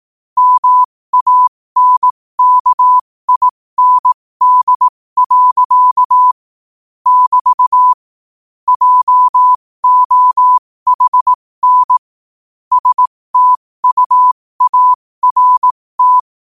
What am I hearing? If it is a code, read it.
Morse code, heard as a keyed tone, about 18 wpm, spaced as for 15 wpm, 1000 Hz. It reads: MANKIND. <BT> JOHN STUART
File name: qotdfiles / 15wpm